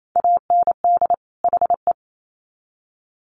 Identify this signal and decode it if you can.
Morse code, heard as a keyed tone, about 28 wpm, 700 Hz. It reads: AND HE
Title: Sets of 2 Words 28wpm